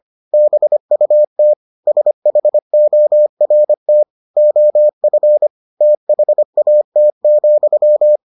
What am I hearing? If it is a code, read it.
Morse code, heard as a keyed tone, about 25 wpm, 600 Hz. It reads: BUT SHORT OF THAT,